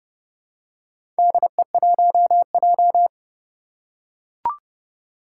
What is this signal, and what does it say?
Morse code, heard as a keyed tone, about 30 wpm, 700 Hz. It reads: DE1J
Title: Four Challenging Letter-Number Combinations 30wpm